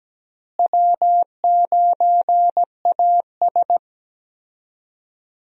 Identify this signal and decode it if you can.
Morse code, heard as a keyed tone, about 17 wpm, 700 Hz. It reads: W9AS
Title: USA Call Signs - 4 Characters - Repeated 3x - 17wpm